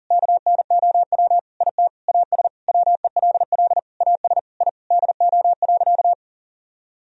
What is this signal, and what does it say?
Morse code, heard as a keyed tone, about 40 wpm, 700 Hz. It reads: KNOW IT AS WELL AS I DO.
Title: Sentences from Top 1000 Words - Review Part 3 - 40wpm